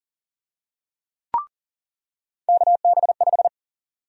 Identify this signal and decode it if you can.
Morse code, heard as a keyed tone, about 40 wpm, 700 Hz. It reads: KB5
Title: USA Call Sign Prefixes - 40wpm